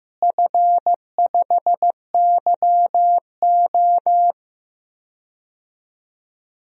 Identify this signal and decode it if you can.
Morse code, heard as a keyed tone, about 15 wpm, 700 Hz. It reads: F5YO
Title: DXCC-100 Call Signs - 4 Characters - Repeated 2x - 15wpm